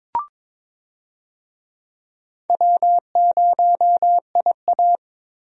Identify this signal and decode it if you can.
Morse code, heard as a keyed tone, about 22 wpm, 700 Hz. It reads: W0IA
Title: USA Call Signs - 4 Characters - Repeated 3x - 22wpm